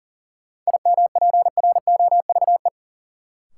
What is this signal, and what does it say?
Morse code, heard as a keyed tone, about 37 wpm, 700 Hz. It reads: IMPROVE
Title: Sentences from Top 800 Words - New Word Drill - 40wpm